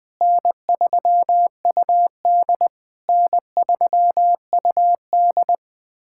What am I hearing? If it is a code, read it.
Morse code, heard as a keyed tone, about 20 wpm, 700 Hz. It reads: N3UD N3UD
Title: POTA QSO Practice - 20wpm